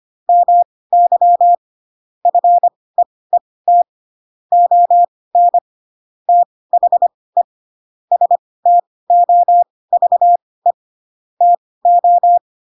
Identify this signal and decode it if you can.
Morse code, heard as a keyed tone, about 25 wpm, spaced as for 18 wpm, 700 Hz. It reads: MY FEET ON THE STOVE TO
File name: Pinocchio / Pinocchio ch7 12 25 ik1hge